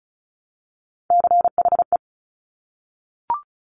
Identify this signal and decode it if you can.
Morse code, heard as a keyed tone, about 36 wpm, 700 Hz. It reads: CHE
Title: Top Three Letter Combinations in Words 35wpm